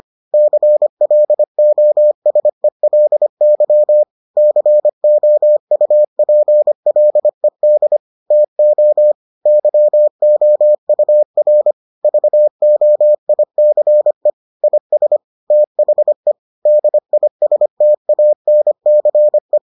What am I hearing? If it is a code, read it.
Morse code, heard as a keyed tone, about 25 wpm, 600 Hz. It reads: CLOSELY COUPLED TO YOUR VOICE IS THE DISTANCE